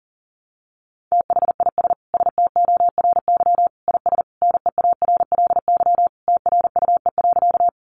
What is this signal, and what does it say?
Morse code, heard as a keyed tone, about 40 wpm, 700 Hz. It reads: THIS STORY IS NEARLY TRUE.
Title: Sentences from Top 1300 Words 40wpm